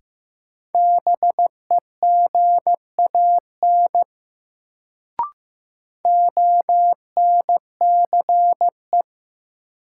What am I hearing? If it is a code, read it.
Morse code, heard as a keyed tone, about 15 wpm, 700 Hz. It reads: BEGAN ONCE
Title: Sentences from Top 300 Words - New Word Drill - 15wpm